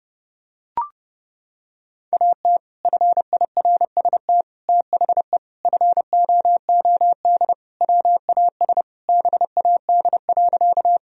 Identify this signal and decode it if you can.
Morse code, heard as a keyed tone, about 30 wpm, 700 Hz. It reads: AT FIRST THE FOOD WAS BAD.
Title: Sentences from Top 1000 Words - Review Part 1 - 30wpm